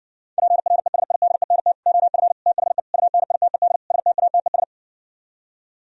Morse code, heard as a keyed tone, about 59 wpm, 700 Hz. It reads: PRESIDENT OF THE UNITED STATES